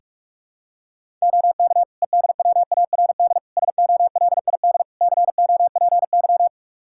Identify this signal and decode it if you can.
Morse code, heard as a keyed tone, about 45 wpm, 700 Hz. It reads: OK EDWARD SOLID COPY
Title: Practice QSO - Exchange 3 - More Info - 45wpm